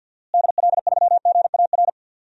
Morse code, heard as a keyed tone, about 52 wpm, 700 Hz. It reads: DL3GAR
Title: DXCC-100 Call Signs - 6 Characters - Repeated 3x - 50wpm